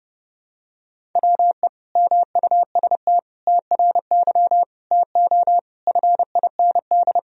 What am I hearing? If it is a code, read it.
Morse code, heard as a keyed tone, about 32 wpm, 700 Hz. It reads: WE MUST TRY TO FIND
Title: Sentences from Top 400 Words and Abbreviations 30wpm